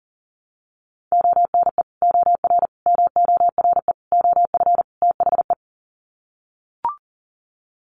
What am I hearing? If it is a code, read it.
Morse code, heard as a keyed tone, about 40 wpm, 700 Hz. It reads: ONE OR MORE OF THE
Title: Sets of 5 Words Encore 40wpm